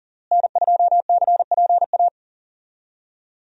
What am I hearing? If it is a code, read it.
Morse code, heard as a keyed tone, about 40 wpm, 700 Hz. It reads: N2CPA
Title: USA Call Signs - 5 Characters - Repeated 3x - 40wpm